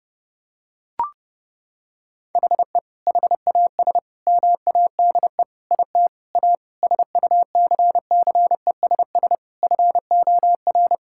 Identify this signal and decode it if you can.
Morse code, heard as a keyed tone, about 32 wpm, 700 Hz. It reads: HE HAS MADE IT A SUCCESS FOR
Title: Sentences from Top 1000 Words - Review Part 6 - 30wpm